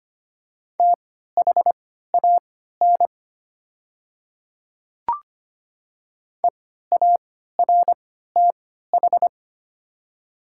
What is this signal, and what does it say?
Morse code, heard as a keyed tone, about 25 wpm, spaced as for 14 wpm, 700 Hz. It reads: THAN EARTH
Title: 3x Character Spacing - Top 200 Words - 25wpm